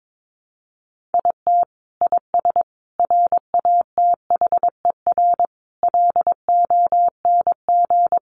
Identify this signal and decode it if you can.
Morse code, heard as a keyed tone, about 22 wpm, 700 Hz. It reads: IT IS RATHER LONG
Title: Sentences from Top 700 Words 22wpm